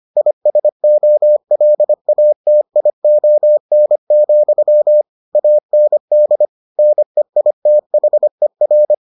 Morse code, heard as a keyed tone, about 25 wpm, 600 Hz. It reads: ISOLATION, AND NEITHER